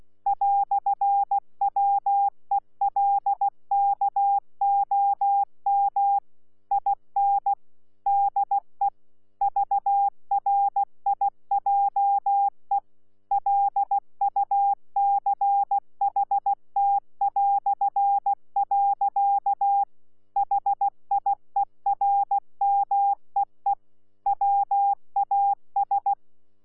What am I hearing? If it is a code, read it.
Morse code, heard as a keyed tone, about 16 wpm, 800 Hz. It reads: "WELKOM IN DE VRIJE LUCHT". HIERMEE WAS
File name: Z PI4VRZ / PI4VRZ 14-jun-2025